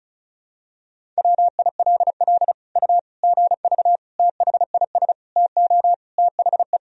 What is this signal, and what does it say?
Morse code, heard as a keyed tone, about 36 wpm, 700 Hz. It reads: WILL U GV THIS TO THE
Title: Sentences from Top 200 Words and Abbreviations 35wpm